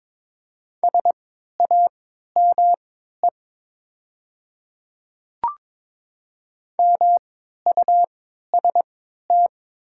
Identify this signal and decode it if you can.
Morse code, heard as a keyed tone, about 22 wpm, spaced as for 13 wpm, 700 Hz. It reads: SAME MUST
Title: 3x Character Spacing - Top 300 Words - 22wpm